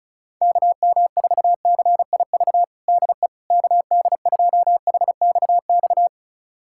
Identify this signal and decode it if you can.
Morse code, heard as a keyed tone, about 36 wpm, 700 Hz. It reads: KM4CIV DE KD2HXX
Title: USA Callsign DE Callsign 35wpm